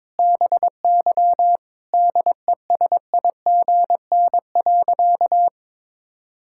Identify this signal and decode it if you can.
Morse code, heard as a keyed tone, about 21 wpm, 700 Hz. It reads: BY DESIGN.
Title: Sentences from Top 1000 Words - Review Part 5 - 22wpm